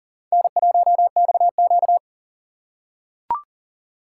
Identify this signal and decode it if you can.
Morse code, heard as a keyed tone, about 40 wpm, 700 Hz. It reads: N1XQ
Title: USA Call Signs - 4 Characters - 4x Character Spacing - 40wpm